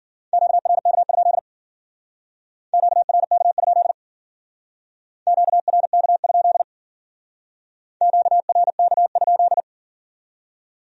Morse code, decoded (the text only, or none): QRK? QRK? QRK? QRK?